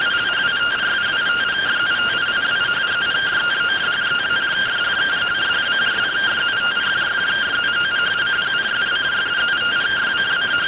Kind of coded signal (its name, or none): two-tone teleprinter signal